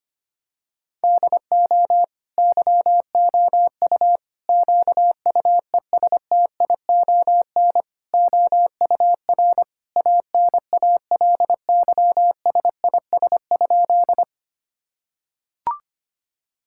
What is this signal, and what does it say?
Morse code, heard as a keyed tone, about 25 wpm, 700 Hz. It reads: DO YOU QUESTION OUR ANALYSIS?
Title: Sentences from Top 1000 Words - Review Part 5 - 25wpm